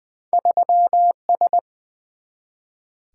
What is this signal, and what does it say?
Morse code, heard as a keyed tone, about 20 wpm, 700 Hz. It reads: 3S
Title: Two Letter-Number Combinations - 20wpm